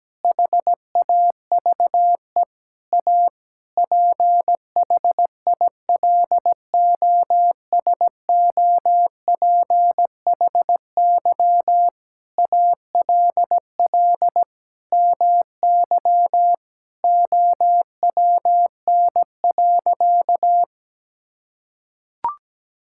Morse code, heard as a keyed tone, about 17 wpm, 700 Hz. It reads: HAVE A PHILOSOPHY ALL MY OWN.
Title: Sentences from Top 1000 Words - Review Part 6 - 17wpm